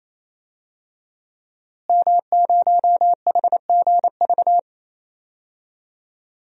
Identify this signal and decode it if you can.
Morse code, heard as a keyed tone, about 28 wpm, 700 Hz. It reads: M0HGV
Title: DXCC-100 Call Signs - 5 Characters - Repeated 3x - 28wpm